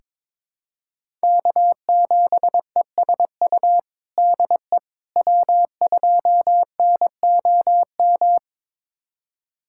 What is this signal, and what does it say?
Morse code, heard as a keyed tone, about 22 wpm, 700 Hz. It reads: K7ESU DE W2NOM
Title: USA Callsign DE Callsign - 5 Characters - 22wpm